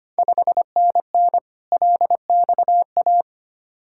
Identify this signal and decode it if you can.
Morse code, heard as a keyed tone, about 25 wpm, 690 Hz. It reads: 5NN LXA
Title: Ontario QSO Party Exchange 25wpm